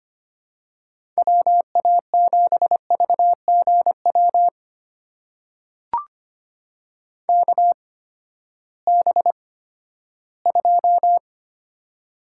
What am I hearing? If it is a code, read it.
Morse code, heard as a keyed tone, about 25 wpm, spaced as for 14 wpm, 700 Hz. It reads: WA7VGW KB2
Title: USA Call Signs - 6 Characters - 8x Character Spacing - 25wpm